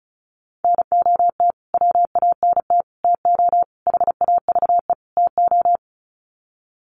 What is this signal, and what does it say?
Morse code, heard as a keyed tone, about 34 wpm, 700 Hz. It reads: NOT WANT TO HAVE TO
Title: Sets of 5 Words Encore2 35wpm